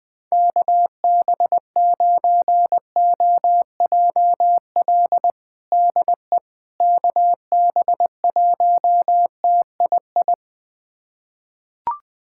Morse code, heard as a keyed tone, about 20 wpm, 700 Hz. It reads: KB9OJL DE KB1TII
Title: USA Callsign DE Callsign 20wpm